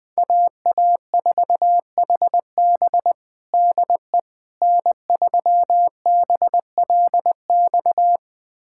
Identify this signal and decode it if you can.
Morse code, heard as a keyed tone, about 20 wpm, 700 Hz. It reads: AA4HB DE N3BLX
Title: USA Callsign DE Callsign - 5 Characters - 20wpm